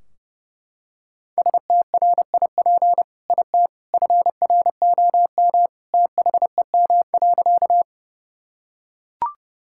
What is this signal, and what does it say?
Morse code, heard as a keyed tone, about 30 wpm, 700 Hz. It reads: STRIP IT FROM THEM.
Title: Sentences from Top 1600 Words 30wpm